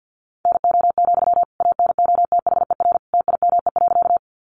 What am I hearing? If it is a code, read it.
Morse code, heard as a keyed tone, about 50 wpm, 700 Hz. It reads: NO, ANOTHER TIME.
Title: Sentences from Top 600 Words 50wpm